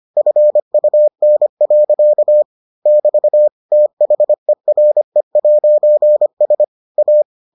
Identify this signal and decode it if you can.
Morse code, heard as a keyed tone, about 25 wpm, 600 Hz. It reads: FUN. <BT> THERE'S A